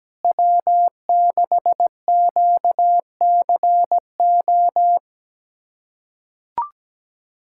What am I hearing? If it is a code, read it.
Morse code, heard as a keyed tone, about 17 wpm, 700 Hz. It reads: W6QCO
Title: USA Callsign DE Callsign 17wpm